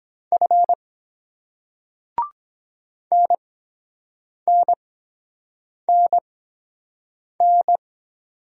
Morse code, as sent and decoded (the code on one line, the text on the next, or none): ..-. / -. -. -. -.
F NNNN